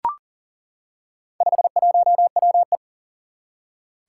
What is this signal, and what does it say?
Morse code, heard as a keyed tone, about 40 wpm, 700 Hz. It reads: 51WE
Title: Four Challenging Letter-Number Combinations 40wpm